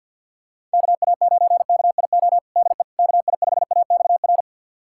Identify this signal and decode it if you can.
Morse code, heard as a keyed tone, about 53 wpm, 700 Hz. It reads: KA9KIO DE KI5AXR
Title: USA Callsign DE Callsign - 6 Characters - 50wpm